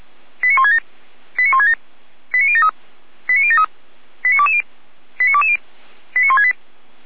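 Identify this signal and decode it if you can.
Five-tone selective call (ZVEI), heard as quick runs of five tones, tones of about 80 ms. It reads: 89167 89167 89073 89073 89220 89220 89168
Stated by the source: ZVEI five-tone selective call received on an SDR.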